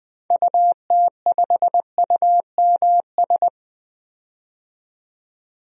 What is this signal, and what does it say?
Morse code, heard as a keyed tone, about 20 wpm, 700 Hz. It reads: UT5UMS
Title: DXCC-100 Call Signs - 6 Characters - 20wpm